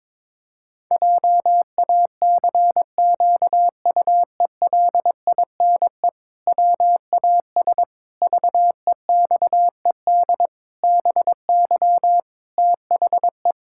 Morse code, heard as a keyed tone, about 22 wpm, 700 Hz. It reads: JACQUELINE WAS VEXED BY THE
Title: Pangram Challenge v2 - 22wpm